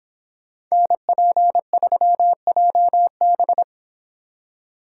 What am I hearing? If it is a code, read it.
Morse code, heard as a keyed tone, about 26 wpm, 700 Hz. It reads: NP3JB